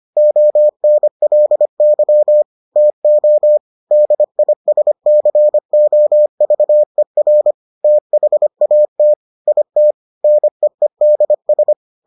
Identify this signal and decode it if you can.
Morse code, heard as a keyed tone, about 25 wpm, 600 Hz. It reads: ONLY TO DISCOVER THAT IT NEEDS